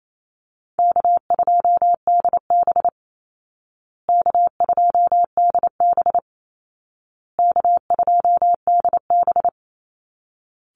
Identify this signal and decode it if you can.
Morse code, heard as a keyed tone, about 28 wpm, 700 Hz. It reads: K2DB K2DB K2DB
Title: Easy Call Signs Repeated 3x - 28wpm